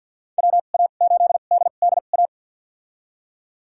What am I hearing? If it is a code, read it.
Morse code, heard as a keyed tone, about 49 wpm, spaced as for 36 wpm, 700 Hz. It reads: WA8DDA
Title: USA Call Signs - 2x Character Spacing - 50wpm